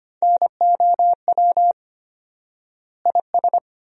Morse code, heard as a keyed tone, about 25 wpm, 700 Hz. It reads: NOW IS
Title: 4x Word-Spacing - Sentences from Top 100 Words - 25wpm